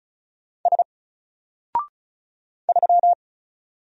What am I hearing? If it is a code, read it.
Morse code, heard as a keyed tone, about 36 wpm, 700 Hz. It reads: S 3
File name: Single-Letter-Number-35wpm